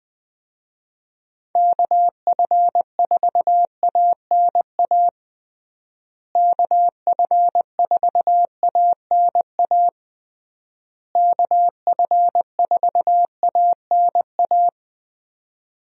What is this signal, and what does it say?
Morse code, heard as a keyed tone, about 21 wpm, 700 Hz. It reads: KF4ANA KF4ANA KF4ANA
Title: USA Call Signs - 6 Characters - Repeated 3x - 20wpm